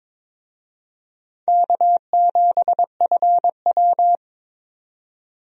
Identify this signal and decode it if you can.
Morse code, heard as a keyed tone, about 22 wpm, 700 Hz. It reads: K7FW
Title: USA Call Signs - 4 Characters - 22wpm